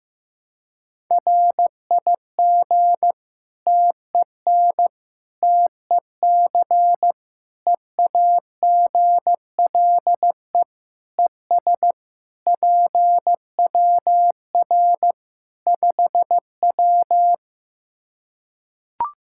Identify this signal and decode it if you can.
Morse code, heard as a keyed tone, about 15 wpm, 700 Hz. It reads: RIG TEN TEC EAGLE ES PWR 5W
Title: Practice QSO - Exchange 2 - Info - 15wpm